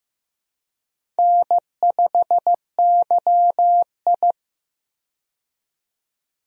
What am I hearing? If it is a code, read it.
Morse code, heard as a keyed tone, about 15 wpm, 700 Hz. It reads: N5YI
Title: USA Call Signs - 4 Characters - 15wpm